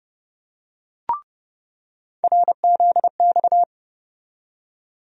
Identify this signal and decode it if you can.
Morse code, heard as a keyed tone, about 30 wpm, 700 Hz. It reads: RZX
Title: USA Call Sign Postfixes - 3 Letters 30wpm